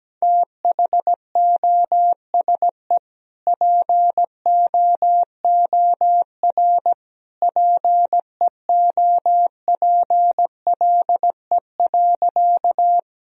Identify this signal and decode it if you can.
Morse code, heard as a keyed tone, about 17 wpm, 700 Hz. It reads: THOSE POOR PEOPLE.
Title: Sentences from Top 1000 Words - Review Part 1 - 17wpm